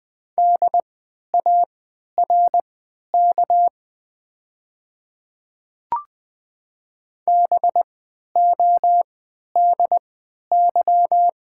Morse code, heard as a keyed tone, about 20 wpm, spaced as for 11 wpm, 700 Hz. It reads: DARK BODY
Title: 3x Character Spacing - Top 500 Words - 20wpm